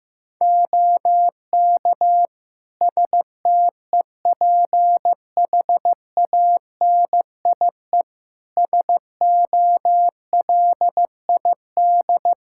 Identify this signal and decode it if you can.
Morse code, heard as a keyed tone, about 15 wpm, 700 Hz. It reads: OK STEPHANIE SOLID
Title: Practice QSO - Exchange 3 - More Info - 15wpm